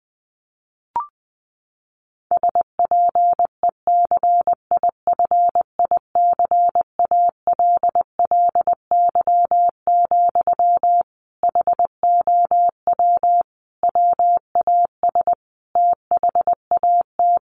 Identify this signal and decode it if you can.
Morse code, heard as a keyed tone, about 20 wpm, 700 Hz. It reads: SPECIFICALLY, HOW WAS THAT
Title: Sentences from Top 1100 Words 20wpm